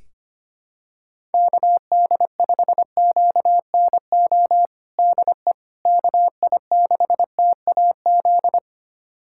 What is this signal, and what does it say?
Morse code, heard as a keyed tone, about 25 wpm, 700 Hz. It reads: KD5QNO DE KI6TAZ
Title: USA Callsign DE Callsign 25wpm